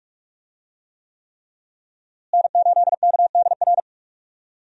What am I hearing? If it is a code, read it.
Morse code, heard as a keyed tone, about 44 wpm, 700 Hz. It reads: N8KDR
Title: USA Call Signs - 5 Characters - Repeated 3x - 45wpm